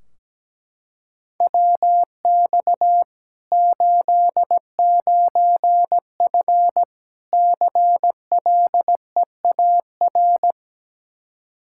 Morse code, heard as a keyed tone, about 17 wpm, 700 Hz. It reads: WX 89F CLEAR
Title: Practice QSO - Exchange 2 - Info - 17wpm